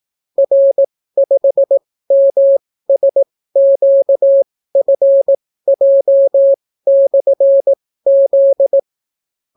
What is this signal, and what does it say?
Morse code, heard as a keyed tone, about 18 wpm, spaced as for 15 wpm, 550 Hz. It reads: R5MSQFJ/Z